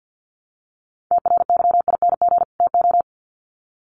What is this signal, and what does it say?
Morse code, heard as a keyed tone, about 50 wpm, 700 Hz. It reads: TRYING TO
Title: Sets of 2 Words 50wpm